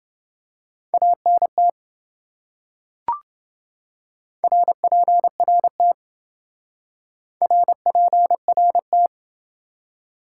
Morse code, heard as a keyed tone, about 27 wpm, 700 Hz. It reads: ANT RPRT RPRT